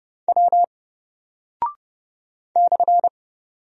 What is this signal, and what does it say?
Morse code, heard as a keyed tone, about 30 wpm, 700 Hz. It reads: W /
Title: Single Character (Letter:Number:Punctuation:Prosign) Alt 30wpm